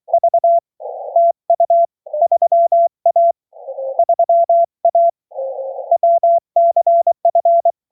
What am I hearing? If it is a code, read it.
Morse code, heard as a keyed tone, about 23 wpm, 690 Hz. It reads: V TU 3A 3A WCF